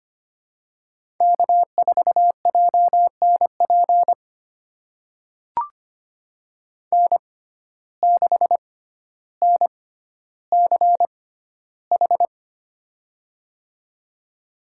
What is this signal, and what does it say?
Morse code, heard as a keyed tone, about 25 wpm, spaced as for 13 wpm, 700 Hz. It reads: K4JNP N6NCH
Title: USA Call Signs - 6x Character Spacing - 25wpm